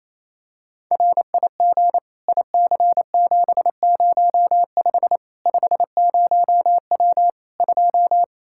Morse code, heard as a keyed tone, about 28 wpm, 700 Hz. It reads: RIG IC705 50W 2
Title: Practice QSO - Exchange 2 - Info - 28wpm